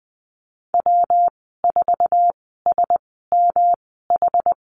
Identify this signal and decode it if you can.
Morse code, heard as a keyed tone, about 20 wpm, spaced as for 15 wpm, 700 Hz. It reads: W4SMH
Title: 122 Lesson 20g - Number 3 - Call Signs 20wpm